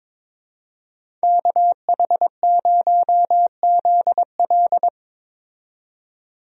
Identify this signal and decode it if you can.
Morse code, heard as a keyed tone, about 22 wpm, 700 Hz. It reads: KH0ZL
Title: USA Call Signs - 5 Characters - 22wpm